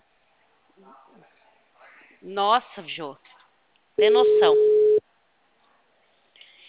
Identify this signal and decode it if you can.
Ringback tone (Europe, 425 Hz), with speech.
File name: out- -2024- - - .387